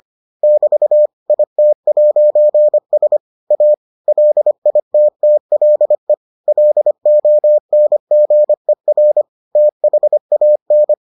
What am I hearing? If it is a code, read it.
Morse code, heard as a keyed tone, about 25 wpm, 600 Hz. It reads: <BT> IT'S A LITTLE LONGER THAN